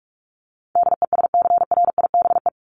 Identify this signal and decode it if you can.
Morse code, heard as a keyed tone, about 44 wpm, 700 Hz. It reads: DESCRIBE
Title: Top 751-1000 Words 45wpm